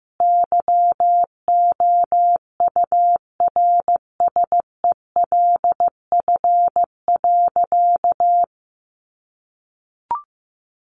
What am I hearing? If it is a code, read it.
Morse code, heard as a keyed tone, about 15 wpm, 700 Hz. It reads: YOURSELF.
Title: Sentences from Top 600 Words 15wpm